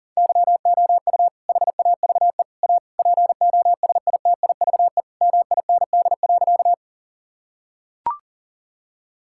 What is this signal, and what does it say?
Morse code, heard as a keyed tone, about 40 wpm, 700 Hz. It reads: YOU HAVE A POSITIVE MIND.
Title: Sentences from Top 1000 Words - Review Part 5 - 40wpm